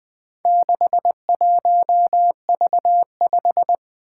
Morse code, heard as a keyed tone, about 21 wpm, 700 Hz. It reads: 61V5
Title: Four Challenging Letter-Number Combinations 20wpm